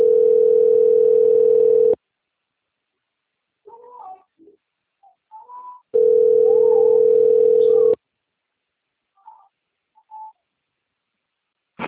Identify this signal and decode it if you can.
Ringback tone (North America), 440 + 480 Hz.